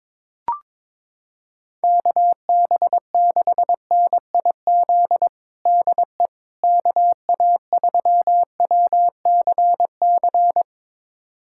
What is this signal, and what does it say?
Morse code, heard as a keyed tone, about 22 wpm, 700 Hz. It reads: KB6NIZ DE KA3WCC
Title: USA Callsign DE Callsign - 6 Characters - 22wpm